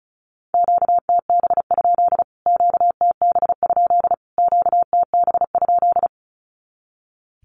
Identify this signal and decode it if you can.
Morse code, heard as a keyed tone, about 34 wpm, 700 Hz. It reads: QTB? QTB? QTB?
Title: Q-Codes Repeated 3x 35wpm